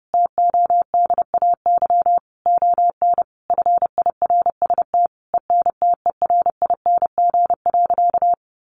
Morse code, heard as a keyed tone, about 30 wpm, 700 Hz. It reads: TODAY ON FIRST ENTERING.
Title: A Sound Of Thunder 30wpm